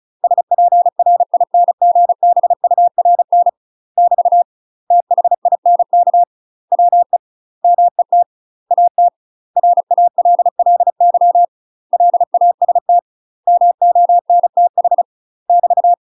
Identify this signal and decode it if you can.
Morse code, heard as a keyed tone, about 34 wpm, 700 Hz. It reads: SPRINGBURN <BT> THINK WE MET AT RALLY LAST MONTH <BT>
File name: UK MASTER QSOs TRAINING / 35WPM / UK MASTER QSOs 0002